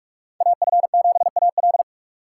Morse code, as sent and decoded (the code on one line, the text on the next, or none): .- ..-. --... .- .-..
AF7AL